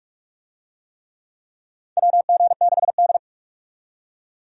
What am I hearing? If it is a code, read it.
Morse code, heard as a keyed tone, about 46 wpm, 700 Hz. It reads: WG6D
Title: USA Call Signs - 4 Characters - 45wpm